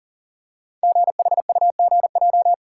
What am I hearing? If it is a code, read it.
Morse code, heard as a keyed tone, about 38 wpm, 700 Hz. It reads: GHUGJ
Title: Code Groups - 5 Letters - 40wpm